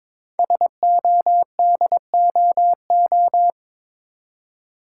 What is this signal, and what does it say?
Morse code, heard as a keyed tone, about 22 wpm, 700 Hz. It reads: SODOO
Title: Code Groups - 5 Letters - 22wpm